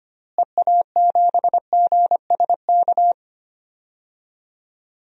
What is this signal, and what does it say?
Morse code, heard as a keyed tone, about 25 wpm, 700 Hz. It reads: EA7GSK
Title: DXCC-100 Call Signs - 6 Characters - Repeated 3x - 25wpm